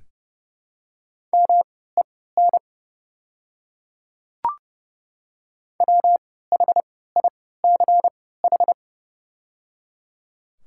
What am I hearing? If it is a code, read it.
Morse code, heard as a keyed tone, about 30 wpm, spaced as for 17 wpm, 700 Hz. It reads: MEN WHICH
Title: 3x Character Spacing - Top 200 Words - 30wpm